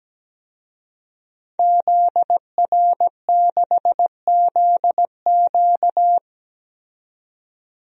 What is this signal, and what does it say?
Morse code, heard as a keyed tone, about 17 wpm, 700 Hz. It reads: ZR6ZQ
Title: DXCC-100 Call Signs - 5 Characters - 17wpm